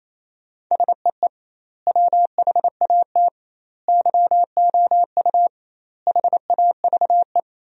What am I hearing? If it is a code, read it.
Morse code, heard as a keyed tone, about 28 wpm, 700 Hz. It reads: SEE WHAT YOU HAVE
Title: 2x Word-Spacing - Sentences from Top 500 Words - 28wpm